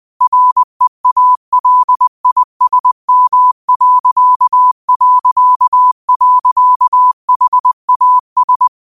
Morse code, heard as a keyed tone, about 20 wpm, 1000 Hz. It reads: REALISM...HAS
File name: qotdfiles / 20wpm